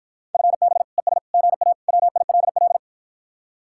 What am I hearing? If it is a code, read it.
Morse code, heard as a keyed tone, about 52 wpm, 700 Hz. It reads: FB ES GA WILL